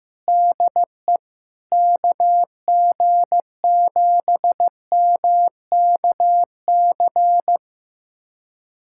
Morse code, heard as a keyed tone, about 15 wpm, 700 Hz. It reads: DE KG7MKC
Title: USA Callsign DE Callsign - 6 Characters - 15wpm